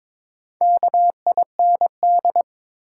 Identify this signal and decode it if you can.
Morse code, heard as a keyed tone, about 22 wpm, 700 Hz. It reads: KIND
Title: Sets of 2 Words - New Word Drill - 22wpm